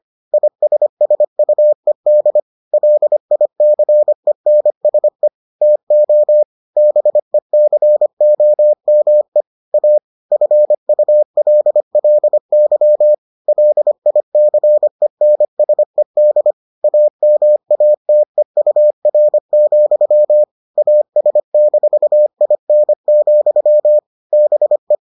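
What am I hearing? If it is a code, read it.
Morse code, heard as a keyed tone, about 25 wpm, 600 Hz. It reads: ISSUED LICENSE TO BECOME A FULLY LICENSED AMATEUR, AS-IN, BE
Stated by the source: Morse edition of the Foundations of Amateur Radio episodes.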